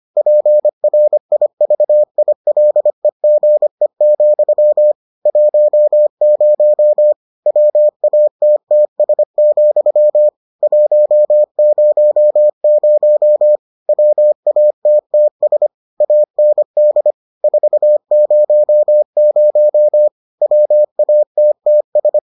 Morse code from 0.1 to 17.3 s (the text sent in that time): PRIVILEGE, 10 WATTS, 100 WATTS AND